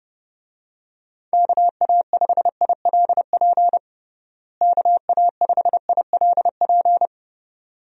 Morse code, heard as a keyed tone, about 30 wpm, 700 Hz. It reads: KA5ILP KA5ILP
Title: USA Call Signs - 6 Characters - Repeated 2x - 30wpm